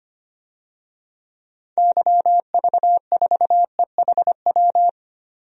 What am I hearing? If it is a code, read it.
Morse code, heard as a keyed tone, about 25 wpm, 700 Hz. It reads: YV4EHW
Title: DXCC-100 Call Signs - 6 Characters - 25wpm